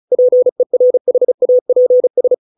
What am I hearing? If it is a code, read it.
Morse code, heard as a keyed tone, about 34 wpm, 500 Hz. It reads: PERHAPS